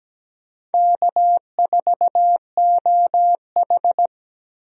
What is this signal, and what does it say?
Morse code, heard as a keyed tone, about 17 wpm, 700 Hz. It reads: K4OH
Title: USA Call Signs - 4 Characters - 17wpm